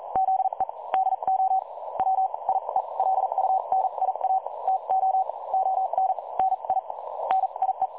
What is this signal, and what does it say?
Morse code, heard as a keyed tone, about 38 wpm, spaced as for 32 wpm, 760 Hz. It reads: ZE DO WIELUNIA TO ON NI NIE
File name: dlazaawansowanych / nagraniazpasma / websdr recording 3565.0kHz